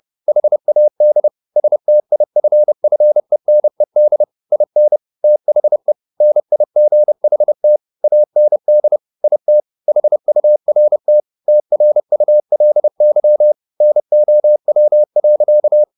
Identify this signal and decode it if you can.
Morse code, heard as a keyed tone, about 30 wpm, 600 Hz. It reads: HAD STIFFENED IN THE NIGHT AND IT HURT TRULY NOW.
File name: the old man and the sea / the old man and the sea 1 30wpm